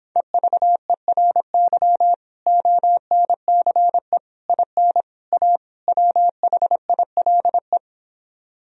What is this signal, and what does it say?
Morse code, heard as a keyed tone, about 26 wpm, 700 Hz. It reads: EVERY ONCE IN A WHILE